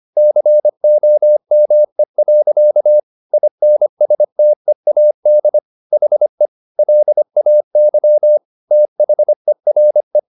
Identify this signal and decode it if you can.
Morse code, heard as a keyed tone, about 25 wpm, 600 Hz. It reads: COME. INSTEAD HE LAY THERE